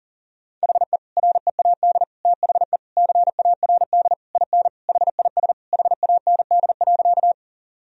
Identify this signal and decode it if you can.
Morse code, heard as a keyed tone, about 40 wpm, 700 Hz. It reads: HE READ THE CARD IN HIS HAND.
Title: Sentences from Top 1000 Words - Review Part 4 - 40wpm